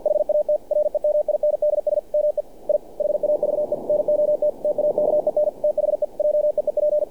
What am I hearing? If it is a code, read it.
Morse code, heard as a keyed tone, about 48 wpm, 600 Hz. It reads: HAT REMINDS ME I HAVE NOT TAKEN THE JEEP
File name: archived / CRSnet